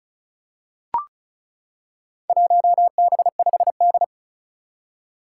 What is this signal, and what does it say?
Morse code, heard as a keyed tone, about 34 wpm, 700 Hz. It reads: 1B5D
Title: Four Challenging Letter-Number Combinations 35wpm